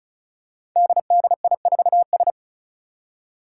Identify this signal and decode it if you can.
Morse code, heard as a keyed tone, about 36 wpm, 700 Hz. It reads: DDI4S
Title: Five Challenging Letter-Number Combinations 35wpm